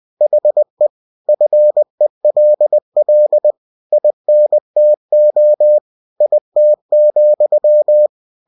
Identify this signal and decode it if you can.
Morse code, heard as a keyed tone, about 20 wpm, 600 Hz. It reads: HE FELL INTO IT,
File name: cw-books / winnie-the-pooh / winnie-20wpm-page-0036